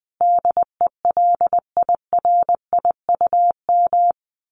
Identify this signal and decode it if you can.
Morse code, heard as a keyed tone, about 20 wpm, 700 Hz. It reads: DELIRIUM
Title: A Sound Of Thunder 20wpm